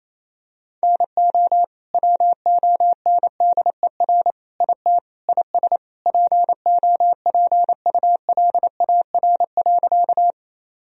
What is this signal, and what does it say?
Morse code, heard as a keyed tone, about 28 wpm, 700 Hz. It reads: NO WONDER IT IS POPULAR.
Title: Sentences from Top 1000 Words - Review Part 3 - 28wpm